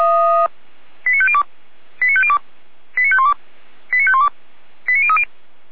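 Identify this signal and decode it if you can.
Five-tone selective call (ZVEI), heard as quick runs of five tones, tones of about 70 ms. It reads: 89502 89502 89512 89512 89030
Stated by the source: ZVEI five-tone selective call received with an SDR for Feuerwehr Orlamünde, a fire brigade.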